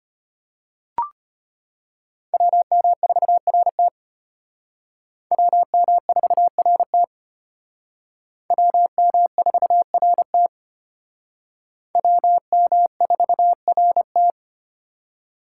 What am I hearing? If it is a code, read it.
Morse code, heard as a keyed tone, about 31 wpm, 700 Hz. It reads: WM4RT WM4RT WM4RT WM4RT
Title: Speed Racing - USA 2x2 Call Signs 25wpm